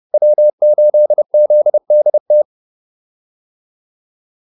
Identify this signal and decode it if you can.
Morse code, heard as a keyed tone, about 30 wpm, 600 Hz. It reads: W8ZDT
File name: QSO108 30